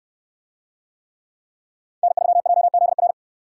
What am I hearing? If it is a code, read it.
Morse code, heard as a keyed tone, about 68 wpm, 700 Hz. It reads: N3JZR